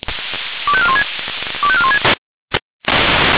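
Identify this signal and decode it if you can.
Five-tone selective call (ZVEI), heard as quick runs of five tones, tones of about 70 ms. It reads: 25416 25416
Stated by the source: Five-tone ZVEI selective call received on an SDR.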